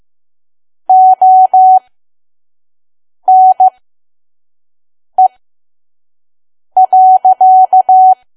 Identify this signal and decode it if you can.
Morse code, heard as a keyed tone, about 15 wpm, spaced as for 5 wpm, 750 Hz. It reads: ONE.
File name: Archive / 5 WPM files / PM